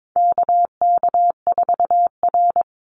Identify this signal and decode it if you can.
Morse code, heard as a keyed tone, about 22 wpm, 700 Hz. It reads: KK4R
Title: Call Signs - 22wpm